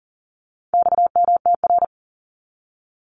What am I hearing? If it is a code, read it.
Morse code, heard as a keyed tone, about 40 wpm, 700 Hz. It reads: XMTR
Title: Abbreviations 40wpm